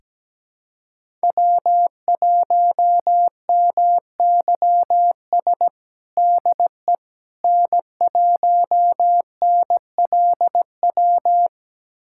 Morse code, heard as a keyed tone, about 17 wpm, 700 Hz. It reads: W1MYS DE N1NLW
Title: USA Callsign DE Callsign - 5 Characters - 17wpm